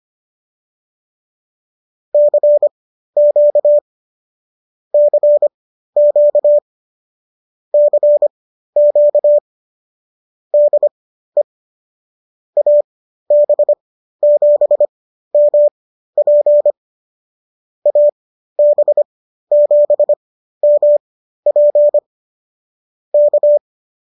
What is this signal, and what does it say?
Morse code, heard as a keyed tone, about 25 wpm, spaced as for 13 wpm, 590 Hz. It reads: CQ CQ CQ DE AB7MP AB7MP K
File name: qso206 13